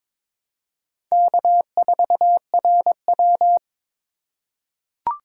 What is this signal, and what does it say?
Morse code, heard as a keyed tone, about 22 wpm, 700 Hz. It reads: K4RW
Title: USA Call Signs - 4 Characters - 2x Character Spacing - 22wpm